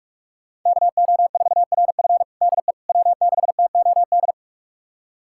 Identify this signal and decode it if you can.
Morse code, heard as a keyed tone, about 46 wpm, 700 Hz. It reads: KO4RF DE W6TOD
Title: USA Callsign DE Callsign - 5 Characters - 45wpm